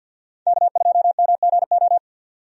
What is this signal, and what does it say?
Morse code, heard as a keyed tone, about 52 wpm, 700 Hz. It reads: K2MGO
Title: USA Call Signs - 5 Characters - 50wpm